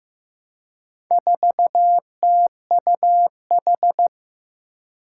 Morse code, heard as a keyed tone, about 15 wpm, 700 Hz. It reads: 4TUH
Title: Four Challenging Letter-Number Combinations 15wpm